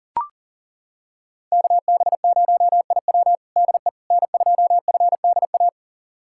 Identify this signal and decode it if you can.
Morse code, heard as a keyed tone, about 40 wpm, 700 Hz. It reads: KB0IW DE N2FDA
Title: USA Callsign DE Callsign 40wpm